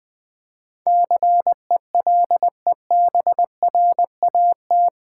Morse code, heard as a keyed tone, about 20 wpm, 700 Hz. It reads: CELEBRAT
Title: Build A Word - 4 Letters At A Time - 11-Letter Words - 20wpm